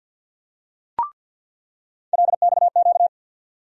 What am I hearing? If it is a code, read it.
Morse code, heard as a keyed tone, about 47 wpm, 700 Hz. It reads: LXQ
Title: USA Call Sign Postfixes - 3 Letters 50wpm